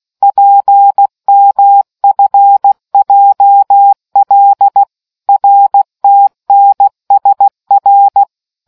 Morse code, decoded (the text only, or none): PMFJL RTNSR